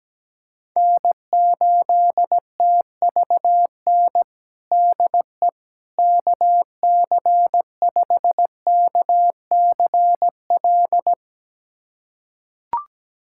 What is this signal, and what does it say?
Morse code, heard as a keyed tone, about 17 wpm, 700 Hz. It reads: N8TVN DE KC5KCL
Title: USA Callsign DE Callsign 17wpm